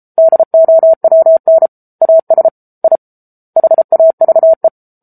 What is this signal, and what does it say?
Morse code, heard as a keyed tone, about 34 wpm, 650 Hz. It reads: DOWN AS I HAVE
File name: Book Two / 07-Chapter 7 - 33 WPM